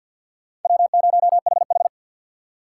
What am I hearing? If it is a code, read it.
Morse code, heard as a keyed tone, about 52 wpm, 700 Hz. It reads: W0HH
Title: USA Call Signs - 4 Characters - 3x Character Spacing - 50wpm